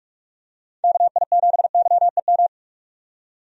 Morse code, heard as a keyed tone, about 44 wpm, 700 Hz. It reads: KI7YEM
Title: USA Call Signs - 6 Characters - 45wpm